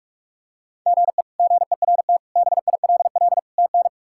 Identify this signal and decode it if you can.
Morse code, heard as a keyed tone, about 42 wpm, 700 Hz. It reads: GE GERT BILL TN
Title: K1USN SST QSO Practice - 45wpm